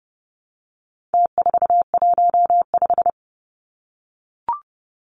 Morse code, heard as a keyed tone, about 30 wpm, 700 Hz. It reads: T415
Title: Numbers - Contest Style - 4-Digit 30wpm